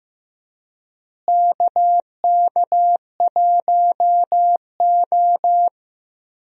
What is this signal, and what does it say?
Morse code, heard as a keyed tone, about 15 wpm, 700 Hz. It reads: KK1O
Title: USA Call Signs - 4 Characters - Repeated 3x - 15wpm